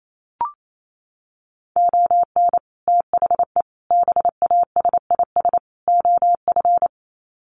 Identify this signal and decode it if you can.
Morse code, heard as a keyed tone, about 28 wpm, 700 Hz. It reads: ON THE BASIS OF
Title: Sets of 4 Words 28wpm